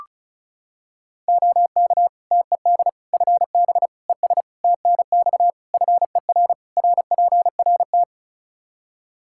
Morse code, heard as a keyed tone, about 34 wpm, 700 Hz. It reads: OK TED FB ES TNX FER RPRT
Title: Practice QSO - Exchange 2 - Info - 35wpm